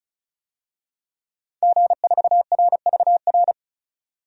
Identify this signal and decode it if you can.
Morse code, heard as a keyed tone, about 34 wpm, 700 Hz. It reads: G4RVR
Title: DXCC-100 Call Signs - 5 Characters - 35wpm